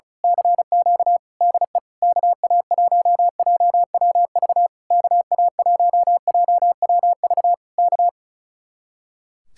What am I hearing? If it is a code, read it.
Morse code, heard as a keyed tone, about 32 wpm, 700 Hz. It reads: CQ DE KA1JWV KA1JWV K
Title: Speed-Racing Common QSO Phrases Encore - 35wpm